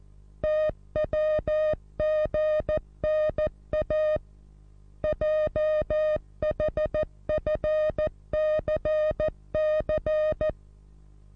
Morse code, heard as a keyed tone, about 14 wpm, 620 Hz. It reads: TWGNA JHFCC
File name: repos / Morsekurs / DJ4UF Morsen27